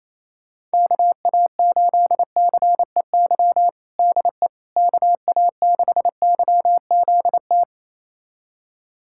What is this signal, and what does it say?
Morse code, heard as a keyed tone, about 28 wpm, 700 Hz. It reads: KA8CEY DE KA6YZT
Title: USA Callsign DE Callsign - 6 Characters - 28wpm